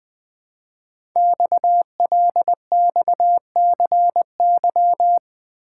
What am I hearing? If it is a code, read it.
Morse code, heard as a keyed tone, about 20 wpm, 700 Hz. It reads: XLXCY
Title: Code Groups - 5 Letters - 20wpm